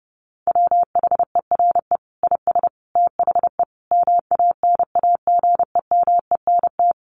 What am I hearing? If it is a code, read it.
Morse code, heard as a keyed tone, about 30 wpm, 700 Hz. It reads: WHERE IS THE MANAGEMENT
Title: Sentences from Top 1200 Words 30wpm